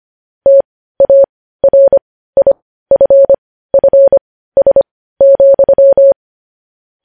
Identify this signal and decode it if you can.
Morse code, heard as a keyed tone, about 25 wpm, spaced as for 15 wpm, 550 Hz. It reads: TARIFFS,
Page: news-headlines-in-morse-code-at-15-wpm